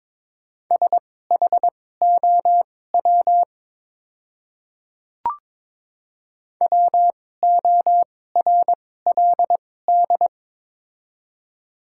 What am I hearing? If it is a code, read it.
Morse code, heard as a keyed tone, about 22 wpm, spaced as for 16 wpm, 700 Hz. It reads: SHOW WORLD
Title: 2x Character Spacing - Top 200 Words - 22wpm